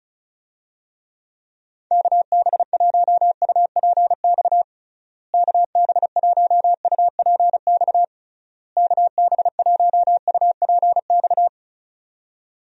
Morse code, heard as a keyed tone, about 36 wpm, 700 Hz. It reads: KB1UPX KB1UPX KB1UPX
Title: USA Call Signs - 6 Characters - Repeated 3x - 35wpm